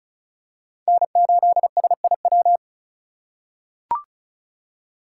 Morse code, heard as a keyed tone, about 36 wpm, 700 Hz. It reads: N8SIW
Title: USA Call Signs - 5 Characters - 2x Character Spacing - 35wpm